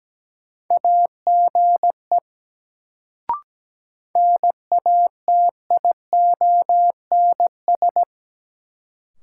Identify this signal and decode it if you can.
Morse code, heard as a keyed tone, about 17 wpm, 700 Hz. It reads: AGE NATIONS
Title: Sets of 3 Words - New Word Drill - 17wpm